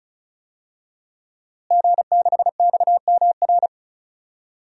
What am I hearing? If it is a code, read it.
Morse code, heard as a keyed tone, about 34 wpm, 700 Hz. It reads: G6XMR
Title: DXCC-100 Call Signs - 5 Characters - Repeated 2x - 35wpm